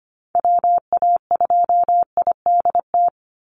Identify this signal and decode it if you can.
Morse code, heard as a keyed tone, about 25 wpm, 700 Hz. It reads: WA2IDT
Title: USA Call Signs 25wpm